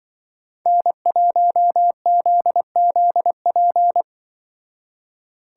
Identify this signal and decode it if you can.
Morse code, heard as a keyed tone, about 25 wpm, 700 Hz. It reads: N1ZZP
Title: Speed Racing - USA 1x3 Call Signs 20wpm